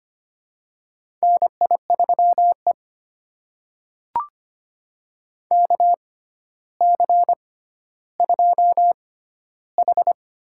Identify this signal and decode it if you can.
Morse code, heard as a keyed tone, about 25 wpm, spaced as for 13 wpm, 700 Hz. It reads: NI3E KC2H
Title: USA Call Signs - 6x Character Spacing - 25wpm